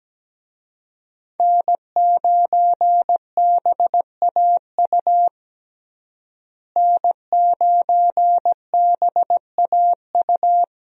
Morse code, heard as a keyed tone, about 17 wpm, 700 Hz. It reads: N9BAU N9BAU
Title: USA Call Signs - 5 Characters - Repeated 3x - 17wpm